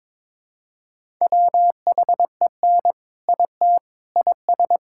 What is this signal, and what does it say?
Morse code, heard as a keyed tone, about 22 wpm, 700 Hz. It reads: WHEN IT IS
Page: morse-code-ninja-practice-22wpm